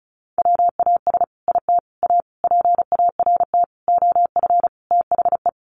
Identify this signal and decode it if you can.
Morse code, heard as a keyed tone, about 34 wpm, 700 Hz. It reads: WAS IT A PART OF THE
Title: Sentences from Top 1500 Words 35wpm